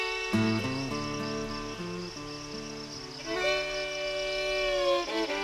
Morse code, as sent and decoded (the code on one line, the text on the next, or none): none